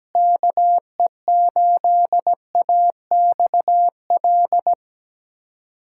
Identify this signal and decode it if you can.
Morse code, heard as a keyed tone, about 17 wpm, 690 Hz. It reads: KE8AXL
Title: SOTA QSO Practice 17wpm